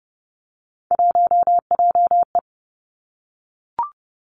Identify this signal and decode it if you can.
Morse code, heard as a keyed tone, about 30 wpm, 700 Hz. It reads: 1JE
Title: Three Challenging Letter-Number Combinations 30wpm